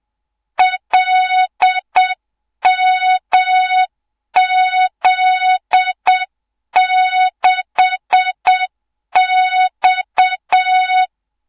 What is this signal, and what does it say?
Morse code, heard as a keyed tone, about 7 wpm, 3700 Hz, with noise. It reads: LMZ6X
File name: Morze-HA2MN / hang / 24 GYAKORLAT